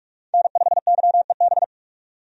Morse code, heard as a keyed tone, about 46 wpm, 700 Hz. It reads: N5YEB
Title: USA Call Signs - 5 Characters - 45wpm